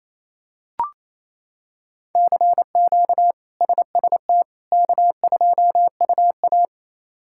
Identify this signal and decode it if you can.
Morse code, heard as a keyed tone, about 28 wpm, 700 Hz. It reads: CQ SST K2UA
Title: K1USN SST QSO Practice - 28wpm